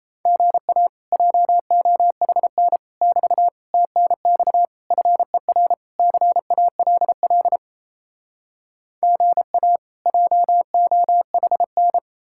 Morse code, heard as a keyed tone, about 37 wpm, 700 Hz. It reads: GA JOHN <BT> TNX FER CALL GA JOHN